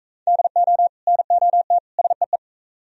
Morse code, heard as a keyed tone, about 45 wpm, 700 Hz. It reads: DO NOT SEE